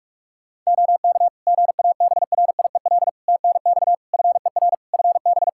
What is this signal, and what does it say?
Morse code, heard as a keyed tone, about 46 wpm, 700 Hz. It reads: OK GABRIEL TNX FER FB
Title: Practice QSO - Exchange 4 - Ending - 45wpm